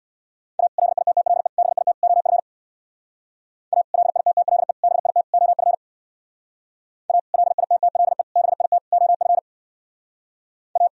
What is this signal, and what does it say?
Morse code, heard as a keyed tone, about 61 wpm, 700 Hz. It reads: A LITTLE BIT OF A LITTLE BIT OF A LITTLE BIT OF A
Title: Speed-Racing - Sets of 4 Words 50wpm